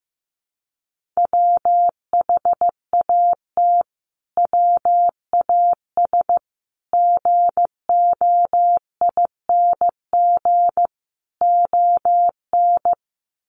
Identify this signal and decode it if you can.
Morse code, heard as a keyed tone, about 15 wpm, 700 Hz. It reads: WHAT WAS GOING ON
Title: Sets of 5 Words Encore 15wpm